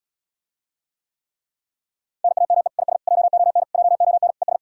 Morse code, heard as a keyed tone, about 80 wpm, 700 Hz. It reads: NAME IS JON JON ES